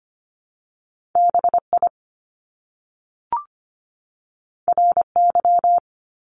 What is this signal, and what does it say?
Morse code, heard as a keyed tone, about 25 wpm, 700 Hz. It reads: BI RY
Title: Top Two Letter Combinations in Words 25wpm